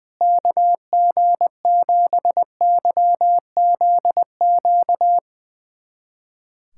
Morse code, heard as a keyed tone, about 20 wpm, 700 Hz. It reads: KG7YZQ
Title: USA Callsign DE Callsign - 6 Characters - 20wpm